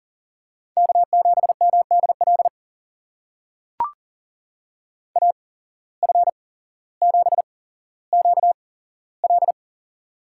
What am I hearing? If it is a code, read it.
Morse code, heard as a keyed tone, about 39 wpm, spaced as for 17 wpm, 700 Hz. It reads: K7MDL AF7QL
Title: USA Call Signs - 5 Characters - 8x Character Spacing - 40wpm